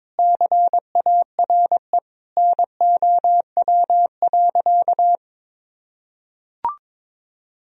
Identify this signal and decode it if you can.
Morse code, heard as a keyed tone, about 22 wpm, 700 Hz. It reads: CARE NOW.
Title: Sentences from Top 1000 Words - Review Part 1 - 22wpm